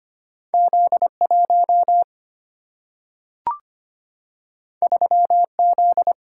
Morse code, heard as a keyed tone, about 25 wpm, 700 Hz. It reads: Z1 3Z
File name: Two Letter-Number Combinations - 25wpm